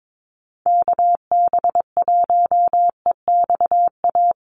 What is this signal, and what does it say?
Morse code, heard as a keyed tone, about 22 wpm, 700 Hz. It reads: KB1EXA
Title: USA Call Signs Repeated 3x 22wpm